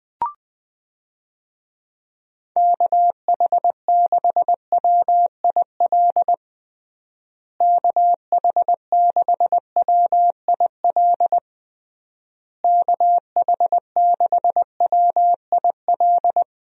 Morse code, heard as a keyed tone, about 20 wpm, 700 Hz. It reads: KH6WIL KH6WIL KH6WIL
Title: USA Call Signs - 6 Characters - Repeated 3x - 20wpm